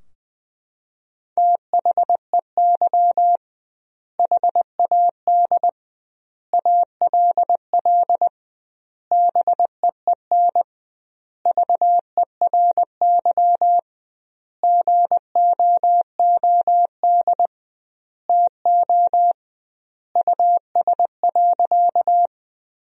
Morse code, heard as a keyed tone, about 20 wpm, 700 Hz. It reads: THEY HAD ALL BEEN VERY GOOD TO US.
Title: 2x Word-Spacing - Sentences from Top 200 Words - 20wpm